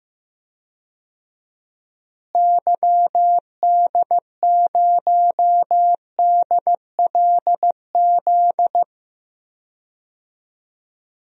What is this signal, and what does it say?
Morse code, heard as a keyed tone, about 15 wpm, 700 Hz. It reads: YD0DLZ
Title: DXCC-100 Call Signs - 6 Characters - Repeated 3x - 15wpm